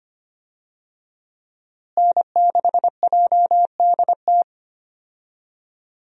Morse code, heard as a keyed tone, about 25 wpm, 700 Hz. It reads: N6JDT
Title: USA Call Signs - 5 Characters - Repeated 3x - 25wpm